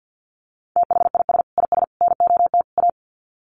Morse code, heard as a keyed tone, about 48 wpm, 700 Hz. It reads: THIS IS NOT A
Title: Sets of 4 Words 50wpm